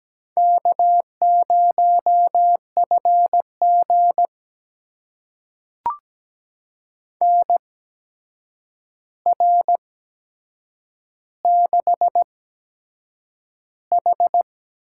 Morse code, heard as a keyed tone, about 17 wpm, spaced as for 7 wpm, 700 Hz. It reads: K0FG NR6H
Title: USA Call Signs - 4 Characters - 8x Character Spacing - 17wpm